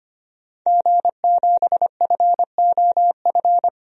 Morse code, heard as a keyed tone, about 25 wpm, 700 Hz. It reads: G7FOF
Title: DXCC-100 Call Signs - 5 Characters - 25wpm